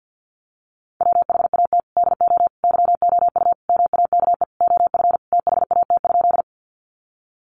Morse code, heard as a keyed tone, about 50 wpm, 700 Hz. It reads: WHAT DO YOU MAKE OF THAT?
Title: Sentences from Top 100 Words 50wpm